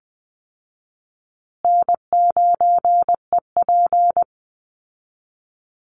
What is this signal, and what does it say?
Morse code, heard as a keyed tone, about 20 wpm, 700 Hz. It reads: N9EP
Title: Easy Call Signs Repeated 3x - 20wpm